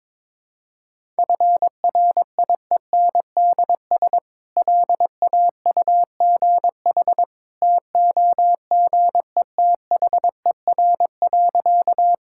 Morse code, heard as a keyed tone, about 22 wpm, 700 Hz. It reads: FRIENDS LAUGH TOGETHER.
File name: practice / 22wpm / Easy Sentences with Hints - 22wpm